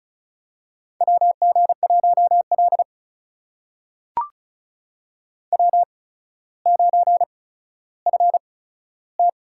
Morse code, heard as a keyed tone, about 36 wpm, spaced as for 15 wpm, 700 Hz. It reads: WG1L W9FT
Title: USA Call Signs - 4 Characters - 8x Character Spacing - 35wpm